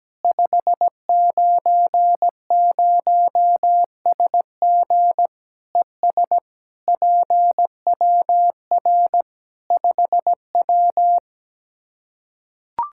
Morse code, heard as a keyed tone, about 17 wpm, 700 Hz. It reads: 590SG ES PWR 5W
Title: Practice QSO - Exchange 2 - Info - 17wpm